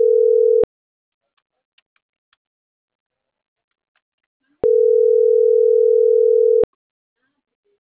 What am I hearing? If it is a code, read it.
Ringback tone (North America), 440 + 480 Hz.